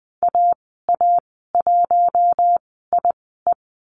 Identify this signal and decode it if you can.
Morse code, heard as a keyed tone, about 20 wpm, spaced as for 14 wpm, 700 Hz. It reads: AA1IE
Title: 048 Lesson 9g - Number 4 - Call Signs 20wpm